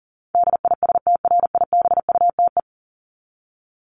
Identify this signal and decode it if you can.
Morse code, heard as a keyed tone, about 40 wpm, 700 Hz. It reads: DISTRIBUTE
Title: Top 1501-1750 Words 40wpm